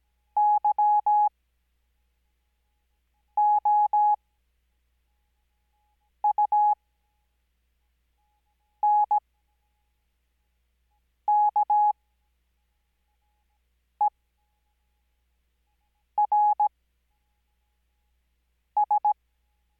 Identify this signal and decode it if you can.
Morse code, heard as a keyed tone, about 17 wpm, spaced as for 4 wpm, 840 Hz. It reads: YOUNKERS
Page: Farnsworth Spacing: 15WPM Character speed Spaced at 5WPM.